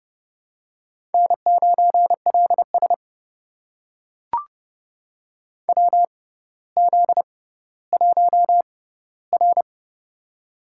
Morse code, heard as a keyed tone, about 32 wpm, spaced as for 16 wpm, 700 Hz. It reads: N9LS WZ1R
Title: USA Call Signs - 6x Character Spacing - 30wpm